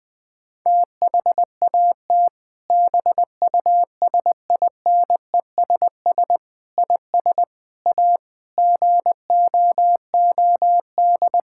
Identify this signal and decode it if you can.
Morse code, heard as a keyed tone, about 20 wpm, 700 Hz. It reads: THAT BUSINESS IS A GOOD
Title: Sentences from Top 1600 Words 20wpm